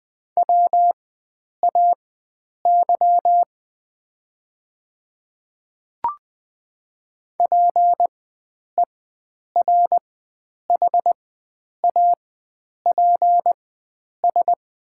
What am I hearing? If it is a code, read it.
Morse code, heard as a keyed tone, about 20 wpm, spaced as for 9 wpm, 700 Hz. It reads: WAY PERHAPS
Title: 4x Character Spacing - Top 500 Words - 20wpm